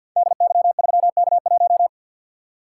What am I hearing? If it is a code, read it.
Morse code, heard as a keyed tone, about 47 wpm, 700 Hz. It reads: DY3K1
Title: Code Groups - 5 Letter:Numbers - 50wpm